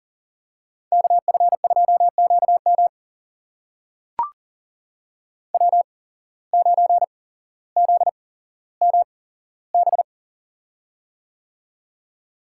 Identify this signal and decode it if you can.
Morse code, heard as a keyed tone, about 39 wpm, spaced as for 17 wpm, 700 Hz. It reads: KF2QM W9ZMB
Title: USA Call Signs - 5 Characters - 8x Character Spacing - 40wpm